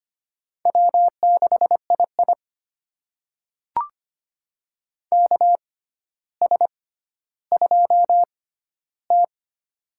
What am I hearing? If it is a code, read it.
Morse code, heard as a keyed tone, about 25 wpm, spaced as for 13 wpm, 700 Hz. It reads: W6II KS2T
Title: USA Call Signs - 6x Character Spacing - 25wpm